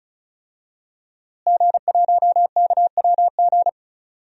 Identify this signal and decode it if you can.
Morse code, heard as a keyed tone, about 36 wpm, 700 Hz. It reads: G1KWG
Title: DXCC-100 Call Signs - 5 Characters - Repeated 2x - 35wpm